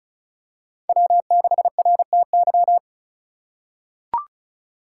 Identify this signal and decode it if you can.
Morse code, heard as a keyed tone, about 34 wpm, 700 Hz. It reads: W6RTY
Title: USA Call Signs - 5 Characters - 3x Character Spacing - 35wpm